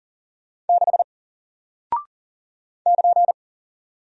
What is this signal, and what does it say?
Morse code, heard as a keyed tone, about 40 wpm, 700 Hz. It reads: 6 <KN>
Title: Single Character (Letter:Number:Punctuation:Prosign) Alt 40wpm